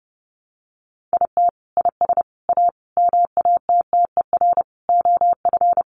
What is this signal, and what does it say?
Morse code, heard as a keyed tone, about 30 wpm, 700 Hz. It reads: IT IS A MATTER OF
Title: Sentences from Top 700 Words 30wpm